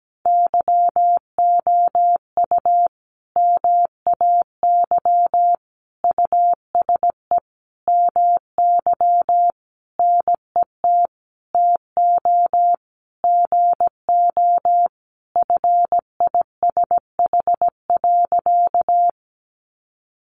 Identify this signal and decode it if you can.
Morse code, heard as a keyed tone, about 17 wpm, 700 Hz. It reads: YOU MAY USE MY NET TO GO FISH.
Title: Sentences from Top 1400 Words 17wpm